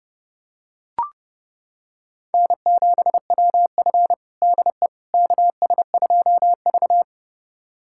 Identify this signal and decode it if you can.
Morse code, heard as a keyed tone, about 30 wpm, 700 Hz. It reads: N7WF DE KS2V
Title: USA Callsign DE Callsign - 4 Characters - 30wpm